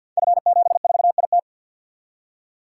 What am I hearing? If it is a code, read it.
Morse code, heard as a keyed tone, about 52 wpm, 700 Hz. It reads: F74IT
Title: Code Groups - 5 Letter:Numbers - 50wpm